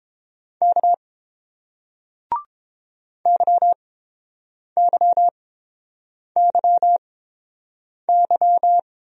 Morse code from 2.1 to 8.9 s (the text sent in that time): YYYY